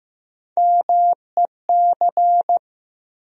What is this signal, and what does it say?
Morse code, heard as a keyed tone, about 15 wpm, 700 Hz. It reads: MEC
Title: USA Call Sign Postfixes - 3 Letters 15wpm